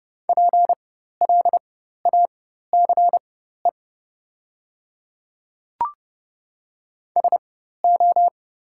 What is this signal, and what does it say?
Morse code, heard as a keyed tone, about 29 wpm, spaced as for 14 wpm, 700 Hz. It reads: PLACE SO
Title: 4x Character Spacing - Top 300 Words - 30wpm